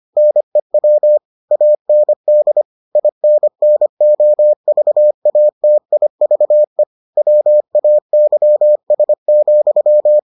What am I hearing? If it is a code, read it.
Morse code, heard as a keyed tone, about 25 wpm, 600 Hz. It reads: NEW AND INNOVATIVE WAYS,